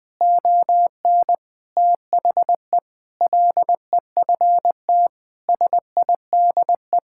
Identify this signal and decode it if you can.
Morse code, heard as a keyed tone, about 21 wpm, 700 Hz. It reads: ON THE LEFT SIDE
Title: Sentences with 6-Word Phrases - 20wpm